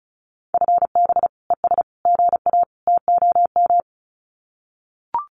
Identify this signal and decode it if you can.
Morse code, heard as a keyed tone, about 34 wpm, 700 Hz. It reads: FB ES GA TOM
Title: Common QSO Phrases Encore 35wpm